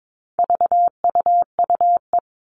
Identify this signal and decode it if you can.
Morse code, heard as a keyed tone, about 22 wpm, 700 Hz. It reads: VUUE
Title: All Cut Numbers - 4-Digit 22wpm